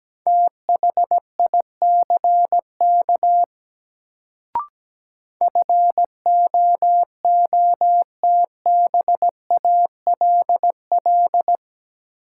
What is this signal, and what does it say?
Morse code, heard as a keyed tone, about 17 wpm, 700 Hz. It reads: THICK FOOTBALL
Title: Sentences from Top 1600 Words - New Word Drill - 17wpm